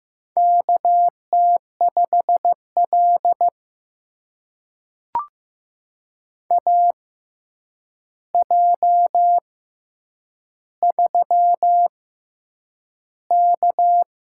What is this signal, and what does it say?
Morse code, heard as a keyed tone, about 15 wpm, spaced as for 8 wpm, 700 Hz. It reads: KT5L AJ3K
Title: USA Call Signs - 4 Characters - 6x Character Spacing - 15wpm